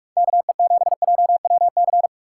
Morse code, heard as a keyed tone, about 49 wpm, 700 Hz. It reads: KE7JWC